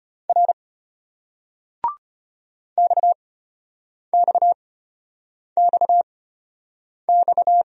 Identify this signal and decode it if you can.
Morse code, heard as a keyed tone, about 33 wpm, spaced as for 8 wpm, 700 Hz. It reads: R XXXX